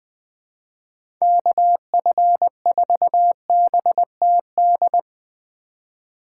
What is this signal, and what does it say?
Morse code, heard as a keyed tone, about 20 wpm, 700 Hz. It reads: KF4BTD
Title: USA Call Signs - 6 Characters - 3x Character Spacing - 20wpm